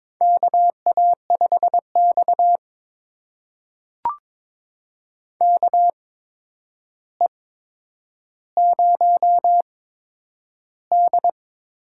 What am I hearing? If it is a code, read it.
Morse code, heard as a keyed tone, about 22 wpm, spaced as for 9 wpm, 700 Hz. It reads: KA5X KE0D
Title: USA Call Signs - 8x Character Spacing - 22wpm